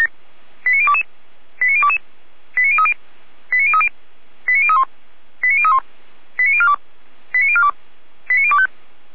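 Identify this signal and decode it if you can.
Five-tone selective call (ZVEI), heard as quick runs of five tones, tones of about 70 ms. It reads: 89022 89022 89030 89030 89031 89031 89042 89042 89025
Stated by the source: ZVEI five-tone call received by SDR.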